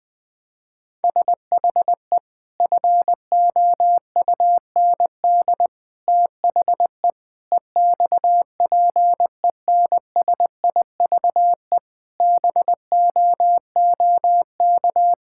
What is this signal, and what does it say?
Morse code, heard as a keyed tone, about 20 wpm, 700 Hz. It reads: SHE FOUND THE EXPENSIVE BOOK
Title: Sentences from Top 1000 Words - Review Part 6 - 20wpm